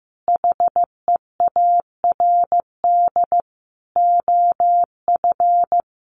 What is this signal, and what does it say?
Morse code, heard as a keyed tone, about 15 wpm, 700 Hz. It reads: HEARD OF
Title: Sentences from Top 1300 Words 15wpm